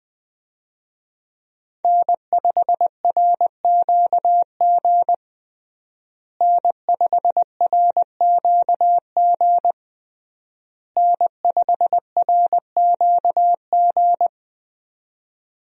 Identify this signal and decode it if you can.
Morse code, heard as a keyed tone, about 21 wpm, 700 Hz. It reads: N5RQG N5RQG N5RQG
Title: USA Call Signs - 5 Characters - Repeated 3x - 20wpm